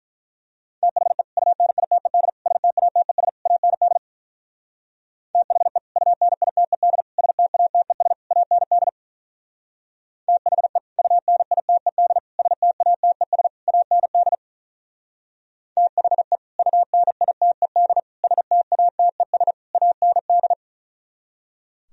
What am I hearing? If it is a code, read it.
Morse code, heard as a keyed tone, about 45 wpm, 700 Hz. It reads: THE UNITED STATES AND THE UNITED STATES AND THE UNITED STATES AND THE UNITED STATES AND